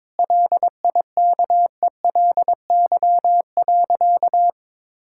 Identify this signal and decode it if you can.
Morse code, heard as a keyed tone, about 22 wpm, 700 Hz. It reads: LIKELY.
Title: Sentences from Top 1600 Words 22wpm